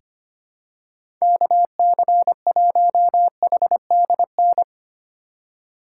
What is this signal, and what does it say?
Morse code, heard as a keyed tone, about 25 wpm, 700 Hz. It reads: KC1HDN
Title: USA Call Signs - 6 Characters - 6x Character Spacing - 25wpm